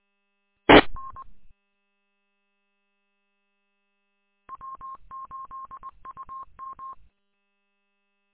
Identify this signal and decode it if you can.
Morse code, heard as a keyed tone, about 20 wpm, 1070 Hz. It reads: TN W8UM
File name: w8um- -combined